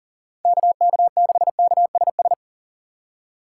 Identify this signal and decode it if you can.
Morse code, heard as a keyed tone, about 40 wpm, 700 Hz. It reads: KK6KSS
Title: USA Call Signs - 4x Character Spacing - 40wpm